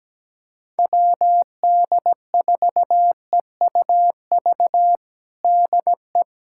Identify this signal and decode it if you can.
Morse code, heard as a keyed tone, about 17 wpm, 700 Hz. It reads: WD4EUV DE
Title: USA Callsign DE Callsign - 6 Characters - 17wpm